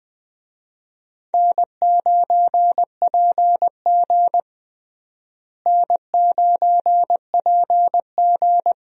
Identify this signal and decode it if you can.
Morse code, heard as a keyed tone, about 21 wpm, 700 Hz. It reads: N9PG N9PG
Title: USA Call Signs - 4 Characters - Repeated 2x - 20wpm